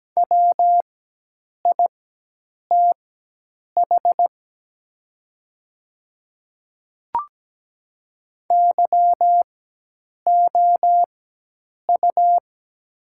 Morse code, heard as a keyed tone, about 17 wpm, spaced as for 8 wpm, 700 Hz. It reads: WITH YOU
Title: 4x Character Spacing - Top 200 Words - 17wpm